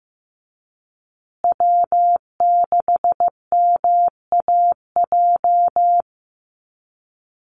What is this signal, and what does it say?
Morse code, heard as a keyed tone, about 14 wpm, 700 Hz. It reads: W6MAJ
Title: USA Call Signs Repeated 3x 15wpm